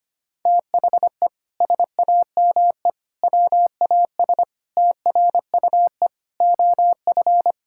Morse code, heard as a keyed tone, about 25 wpm, 700 Hz. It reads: THE SAME WAS TRUE OF
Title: Sentences from Top 1000 Words - Review Part 6 - 25wpm